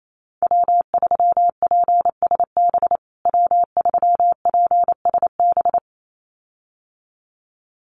Morse code, heard as a keyed tone, about 28 wpm, 700 Hz. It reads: W3PSB W3PSB
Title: USA Call Signs Repeated 2x 28wpm